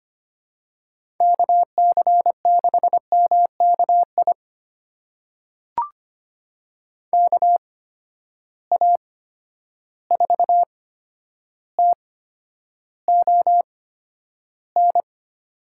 Morse code, heard as a keyed tone, about 25 wpm, spaced as for 11 wpm, 700 Hz. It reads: KC6MKI KA4TON
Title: USA Call Signs - 6 Characters - 8x Character Spacing - 25wpm